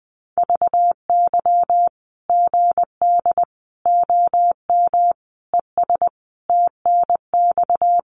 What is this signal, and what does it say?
Morse code, heard as a keyed tone, about 20 wpm, 700 Hz. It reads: VY GD OM ES TNX
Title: Common QSO Phrases 20wpm